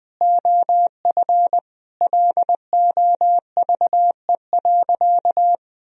Morse code, heard as a keyed tone, about 20 wpm, 700 Hz. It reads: OF LOVE.
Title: Sentences from Top 1000 Words - Review Part 3 - 20wpm